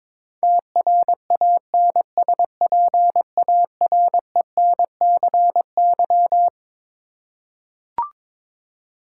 Morse code, heard as a keyed tone, about 21 wpm, 700 Hz. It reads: TRANSPARENCY
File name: Twelve Letter Words - 22wpm